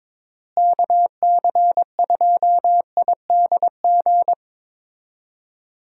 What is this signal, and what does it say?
Morse code, heard as a keyed tone, about 22 wpm, 700 Hz. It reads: KC2IDG
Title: USA Call Signs - 6 Characters - Repeated 2x - 22wpm